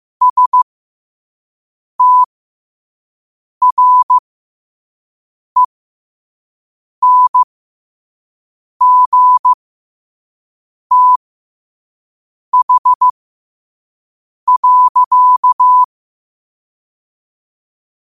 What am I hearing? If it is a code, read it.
Morse code, heard as a keyed tone, about 15 wpm, spaced as for 5 wpm, 1000 Hz. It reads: STRENGTH.
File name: qotdfiles / 5wpm